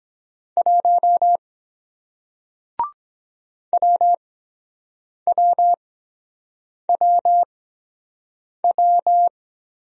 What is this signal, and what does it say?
Morse code, heard as a keyed tone, about 23 wpm, spaced as for 7 wpm, 700 Hz. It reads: 1 WWWW